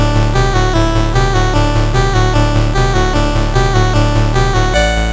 There is no code.